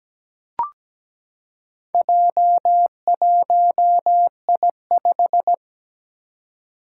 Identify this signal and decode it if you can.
Morse code, heard as a keyed tone, about 17 wpm, 700 Hz. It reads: J1I5
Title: Four Challenging Letter-Number Combinations 17wpm